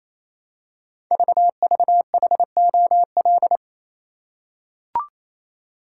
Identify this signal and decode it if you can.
Morse code, heard as a keyed tone, about 27 wpm, 700 Hz. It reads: VVHOL
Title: Code Groups - 5 Letters - 28wpm